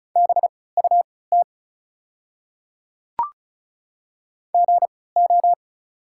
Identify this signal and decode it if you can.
Morse code, heard as a keyed tone, about 36 wpm, spaced as for 20 wpm, 690 Hz. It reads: BUT GO
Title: 3x Character Spacing - Top 100 Words - 35wpm